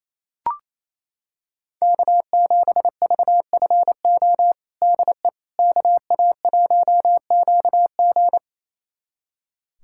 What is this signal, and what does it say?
Morse code, heard as a keyed tone, about 28 wpm, 700 Hz. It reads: K7VFO DE KA1QG
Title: USA Callsign DE Callsign - 5 Characters - 28wpm